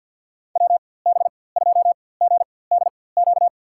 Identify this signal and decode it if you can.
Morse code, heard as a keyed tone, about 53 wpm, spaced as for 24 wpm, 700 Hz. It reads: WB2GDQ
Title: USA Call Signs - 6 Characters - 4x Character Spacing - 50wpm